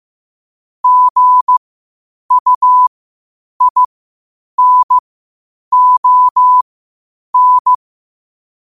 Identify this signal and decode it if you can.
Morse code, heard as a keyed tone, about 15 wpm, spaced as for 8 wpm, 1000 Hz. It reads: GUINON
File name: qotdfiles / 8wpm